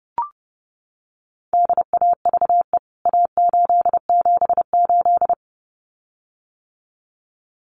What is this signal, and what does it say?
Morse code, heard as a keyed tone, about 29 wpm, 700 Hz. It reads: DAVE A878
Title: CWT Exchange 30wpm